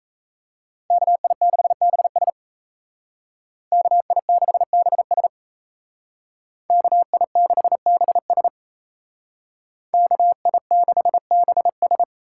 Morse code, heard as a keyed tone, about 36 wpm, 700 Hz. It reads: KI6BS KI6BS KI6BS KI6BS